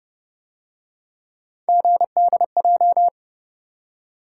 Morse code, heard as a keyed tone, about 32 wpm, 700 Hz. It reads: GDJ